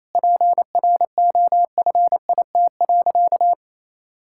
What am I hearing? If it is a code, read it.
Morse code, heard as a keyed tone, about 28 wpm, 700 Hz. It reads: PROFIT.
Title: Sentences from Top 1000 Words - Review Part 5 - 28wpm